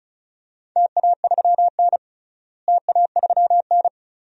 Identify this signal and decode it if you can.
Morse code, heard as a keyed tone, about 36 wpm, 690 Hz. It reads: TA3N TA3N
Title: DXCC-100 Call Signs - 4 Characters - Repeated 2x - 35wpm